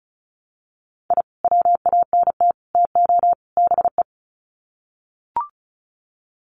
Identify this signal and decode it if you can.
Morse code, heard as a keyed tone, about 34 wpm, 700 Hz. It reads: I WANT TO BE
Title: Sets of 4 Words 35wpm